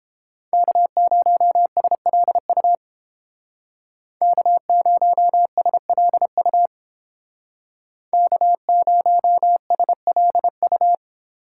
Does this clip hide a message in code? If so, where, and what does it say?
Yes: Morse code from 0.5 to 10.9 s, heard as a keyed tone, about 30 wpm, 700 Hz. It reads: K0SLU K0SLU K0SLU